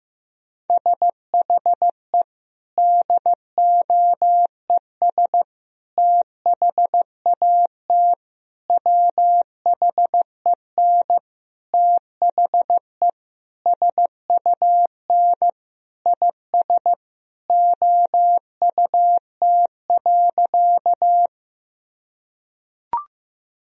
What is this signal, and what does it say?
Morse code, heard as a keyed tone, about 15 wpm, 700 Hz. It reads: SHE DOES THAT WHEN THE SUN IS OUT.
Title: Sentences from Top 1000 Words - Review Part 1 - 15wpm